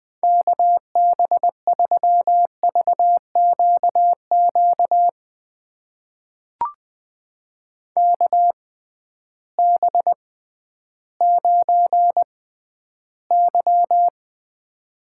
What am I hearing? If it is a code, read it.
Morse code, heard as a keyed tone, about 20 wpm, spaced as for 12 wpm, 700 Hz. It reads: KB3VQQ KB9Y
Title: USA Call Signs - 6x Character Spacing - 20wpm